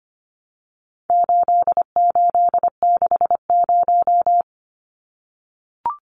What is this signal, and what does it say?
Morse code, heard as a keyed tone, about 25 wpm, 700 Hz. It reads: 8860
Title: Numbers - 4-Digit 25wpm